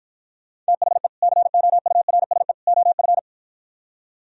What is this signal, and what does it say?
Morse code, heard as a keyed tone, about 52 wpm, 700 Hz. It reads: THE COURSE OF